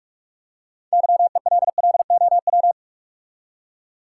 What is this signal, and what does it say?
Morse code, heard as a keyed tone, about 47 wpm, 700 Hz. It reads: YELLOW
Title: Six Letter Words - 45wpm